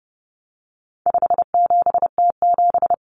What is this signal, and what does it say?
Morse code, heard as a keyed tone, about 30 wpm, 700 Hz. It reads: 57T7
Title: Common Cut Numbers - 4-Digit 30wpm